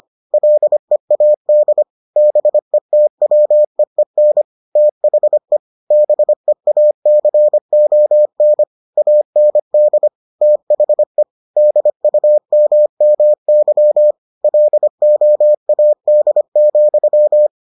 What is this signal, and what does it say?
Morse code, heard as a keyed tone, about 25 wpm, 600 Hz. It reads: LEAD BETWEEN THE BEACON AND THE DUMMY LOAD,